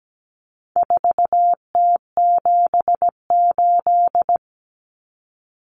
Morse code, heard as a keyed tone, about 17 wpm, 700 Hz. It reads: 4T78
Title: Common Cut Numbers - 4-Digit 17wpm